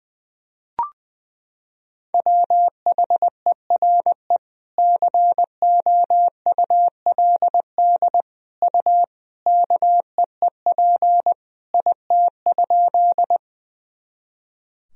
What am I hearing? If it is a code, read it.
Morse code, heard as a keyed tone, about 20 wpm, 700 Hz. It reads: WHERE COULD U KEEP IT?
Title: Sentences from Top 300 Words and Abbreviations 20wpm